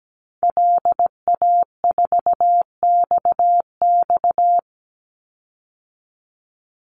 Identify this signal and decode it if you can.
Morse code, heard as a keyed tone, about 17 wpm, 700 Hz. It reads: LA4XX
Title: Call Signs Repeated 2x - 17wpm